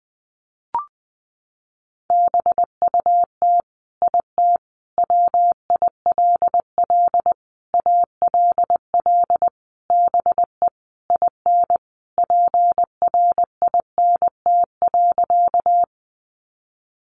Morse code, heard as a keyed tone, about 19 wpm, 700 Hz. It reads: BUT IT WILL ALL BE IN PRINT.
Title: Sentences from Top 1400 Words 20wpm